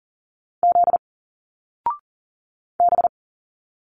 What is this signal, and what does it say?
Morse code, heard as a keyed tone, about 40 wpm, 700 Hz. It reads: Z B
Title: Single Letters 40wpm